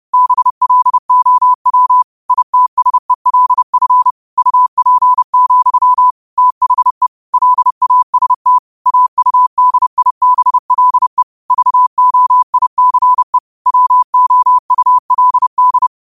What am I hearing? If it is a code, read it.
Morse code, heard as a keyed tone, about 30 wpm, 1000 Hz. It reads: BLOW ITSELF UP, THE LAST AUDIBLE VOICE WOULD
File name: qotdfiles / 30wpm